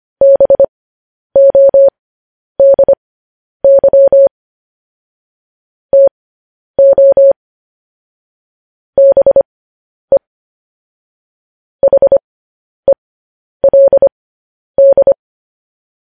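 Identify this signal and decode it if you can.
Morse code, heard as a keyed tone, about 25 wpm, spaced as for 10 wpm, 550 Hz. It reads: BODY TO BE HELD
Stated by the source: Morse code news headlines practice transmission.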